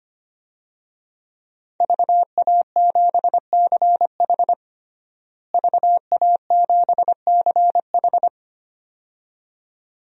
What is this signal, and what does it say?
Morse code, heard as a keyed tone, about 25 wpm, 700 Hz. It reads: VA7CH VA7CH
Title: DXCC-100 Call Signs - 5 Characters - Repeated 2x - 25wpm